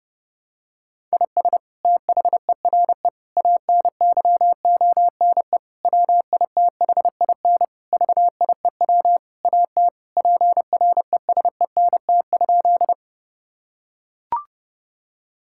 Morse code, heard as a keyed tone, about 32 wpm, 700 Hz. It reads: IS THERE ANYONE WITHIN VIEW AT PRESENT?
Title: Sentences from Top 1000 Words - Review Part 6 - 30wpm